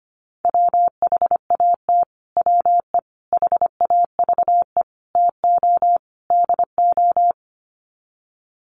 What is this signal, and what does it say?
Morse code, heard as a keyed tone, about 25 wpm, 700 Hz. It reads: WHAT WE HAVE TO DO
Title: Sets of 5 Words Encore 25wpm